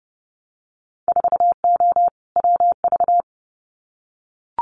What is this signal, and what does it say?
Morse code, heard as a keyed tone, about 30 wpm, 700 Hz. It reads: 4O WV
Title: Winter Field Day Exchanges Encore 30wpm